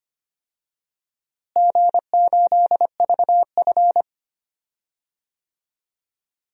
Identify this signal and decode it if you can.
Morse code, heard as a keyed tone, about 25 wpm, 700 Hz. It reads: G8VF
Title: DXCC-100 Call Signs - 4 Characters - Repeated 2x - 25wpm